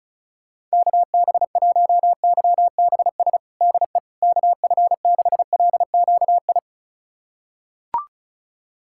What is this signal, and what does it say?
Morse code, heard as a keyed tone, about 36 wpm, 700 Hz. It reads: KB1YBS DE KF6LQI
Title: USA Callsign DE Callsign - 6 Characters - 35wpm